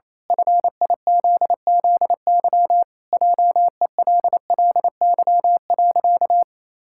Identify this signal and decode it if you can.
Morse code, heard as a keyed tone, about 28 wpm, 700 Hz. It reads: FIZZY JELLY.
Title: Pangram Challenge v2 - 28wpm